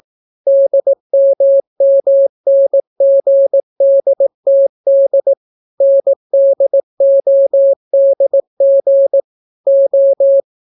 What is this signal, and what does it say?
Morse code, heard as a keyed tone, about 18 wpm, 550 Hz. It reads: DMMNGDTD NDODG O